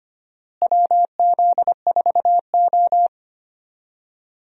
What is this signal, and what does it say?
Morse code, heard as a keyed tone, about 25 wpm, 700 Hz. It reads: WZ4O
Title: USA Call Signs - 4 Characters - Repeated 2x - 25wpm